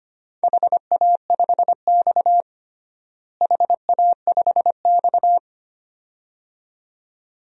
Morse code, heard as a keyed tone, about 25 wpm, 700 Hz. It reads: HA5X HA5X
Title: DXCC-100 Call Signs - 4 Characters - Repeated 2x - 25wpm